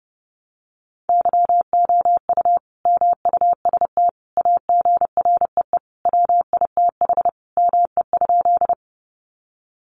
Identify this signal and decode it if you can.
Morse code, heard as a keyed tone, about 28 wpm, 700 Hz. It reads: YOU MUST AGREE WITH ME?
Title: Sentences from Top 1100 Words 30wpm